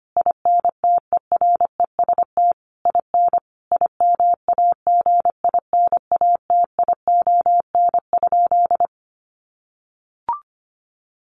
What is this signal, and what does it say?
Morse code, heard as a keyed tone, about 25 wpm, 700 Hz. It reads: INTEREST IN IMAGINATION?
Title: Sentences from Top 1300 Words 25wpm